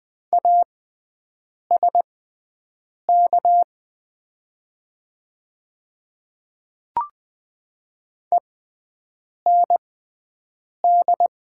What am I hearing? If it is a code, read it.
Morse code, heard as a keyed tone, about 20 wpm, spaced as for 7 wpm, 700 Hz. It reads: ASK END
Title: 6x Character Spacing - Top 200 Words - 20wpm